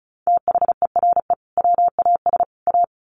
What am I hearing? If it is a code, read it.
Morse code, heard as a keyed tone, about 34 wpm, 700 Hz. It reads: THERE WAS A
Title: Sets of 3 Words 35wpm